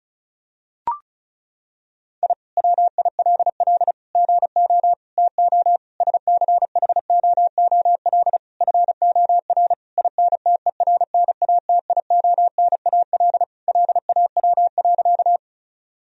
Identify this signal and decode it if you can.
Morse code, heard as a keyed tone, about 36 wpm, 700 Hz. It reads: I WILL GO TO SCHOOL FOR INTERNATIONAL LAW.
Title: Sentences from Top 1000 Words - Review Part 4 - 35wpm